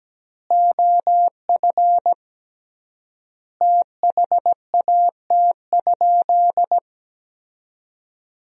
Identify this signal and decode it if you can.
Morse code, heard as a keyed tone, about 17 wpm, 700 Hz. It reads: OF THAT?
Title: 3x Word-Spacing - Sentences from Top 300 Words - 17wpm